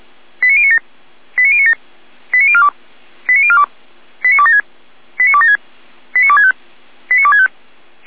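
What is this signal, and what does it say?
Five-tone selective call (ZVEI), heard as quick runs of five tones, tones of about 70 ms. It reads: 89097 89097 89042 89042 89276 89276 89265 89265
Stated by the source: ZVEI five-tone selective call received on SDR.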